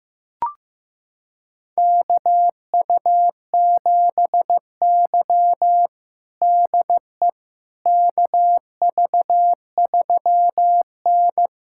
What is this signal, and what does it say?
Morse code, heard as a keyed tone, about 15 wpm, 700 Hz. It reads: KU7Y DE KV3N
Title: USA Callsign DE Callsign - 4 Characters - 15wpm